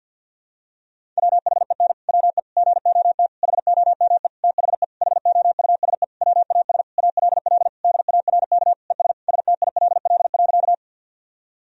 Morse code, heard as a keyed tone, about 52 wpm, 700 Hz. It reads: WHEN WE GOT HOME THE HOUSE WAS ALL DARK ES STILL.
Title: Sentences from Top 500 Words and Abbreviations 50wpm